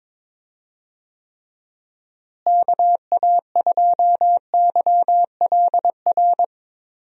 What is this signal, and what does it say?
Morse code, heard as a keyed tone, about 22 wpm, 700 Hz. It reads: KA2YLR
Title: USA Call Signs - 6 Characters - Repeated 3x - 22wpm